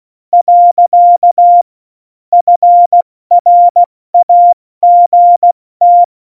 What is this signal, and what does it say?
Morse code, heard as a keyed tone, about 16 wpm, 700 Hz. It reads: . FRAGT
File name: Morsekurs / Witze / Witz 22. Mai 2018, 70 F80